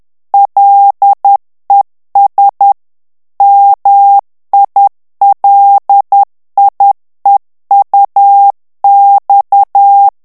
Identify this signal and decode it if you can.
Morse code, heard as a keyed tone, about 10 wpm, 800 Hz. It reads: LES MILIEUX
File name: FRANCAIS TEXTE CLAIR 2 VITESSE 10 MOTS MINUTES